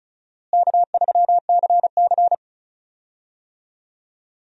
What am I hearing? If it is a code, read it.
Morse code, heard as a keyed tone, about 36 wpm, 700 Hz. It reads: K3CC
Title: USA Call Signs - 4 Characters - Repeated 2x - 35wpm